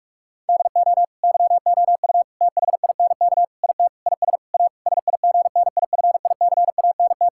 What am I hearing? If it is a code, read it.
Morse code, heard as a keyed tone, about 41 wpm, 700 Hz. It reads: DO YOU THINK IT IS A SIGNIFICANT
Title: Sentences from Top 1000 Words - Review Part 6 - 45wpm